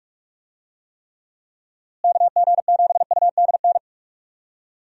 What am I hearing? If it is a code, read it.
Morse code, heard as a keyed tone, about 45 wpm, 700 Hz. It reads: KG7UDN
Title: USA Call Signs - 6 Characters - Repeated 3x - 45wpm